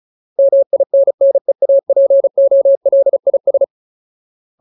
Morse code, heard as a keyed tone, about 34 wpm, 550 Hz. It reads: MINNEAPOLIS
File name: Ultimate Word List / 35WPM Ultimate